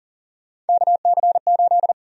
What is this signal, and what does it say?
Morse code, heard as a keyed tone, about 40 wpm, 700 Hz. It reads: KC8
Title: USA Call Sign Prefixes - Repeated 2x - 40wpm